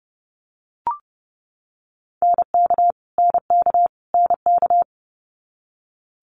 Morse code, heard as a keyed tone, about 30 wpm, 700 Hz. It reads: NK NK NK
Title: Multi-letter Phonemes Repeated 3x 30wpm